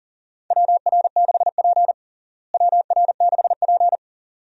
Morse code, heard as a keyed tone, about 40 wpm, 700 Hz. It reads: WR6P WR6P
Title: USA Call Signs - 4 Characters - Repeated 2x - 40wpm